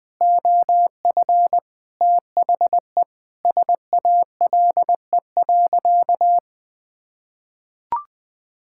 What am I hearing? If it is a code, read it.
Morse code, heard as a keyed tone, about 20 wpm, 700 Hz. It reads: OF THE SALE.
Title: Sentences with 5-Word Phrases - 20wpm